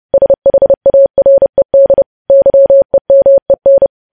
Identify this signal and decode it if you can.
Morse code, heard as a keyed tone, about 30 wpm, 550 Hz. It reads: SHARED YEMEN
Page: News Headlines in Morse Code at 30 WPM